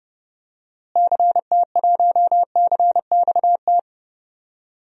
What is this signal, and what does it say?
Morse code, heard as a keyed tone, about 30 wpm, 700 Hz. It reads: CT1CXT
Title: DXCC-100 Call Signs - 6 Characters - Repeated 3x - 30wpm